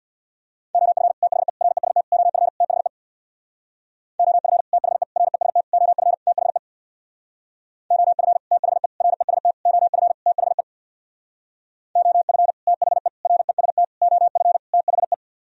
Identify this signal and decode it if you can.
Morse code, heard as a keyed tone, about 62 wpm, 700 Hz. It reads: OF THE REST OF THE OF THE REST OF THE OF THE REST OF THE OF THE REST OF THE
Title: Speed-Racing - Sets of 5 Words 50wpm